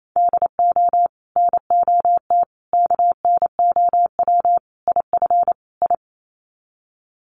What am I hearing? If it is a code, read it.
Morse code, heard as a keyed tone, about 28 wpm, 700 Hz. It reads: DO NOT KNOW IF I
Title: Sets of 5 Words 28wpm